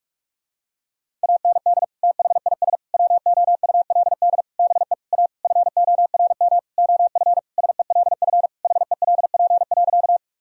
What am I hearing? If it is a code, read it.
Morse code, heard as a keyed tone, about 46 wpm, 700 Hz. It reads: AND THIS WOULD BE A FORM OF SELF HELP.
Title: Sentences from Top 1000 Words - Review Part 1 - 45wpm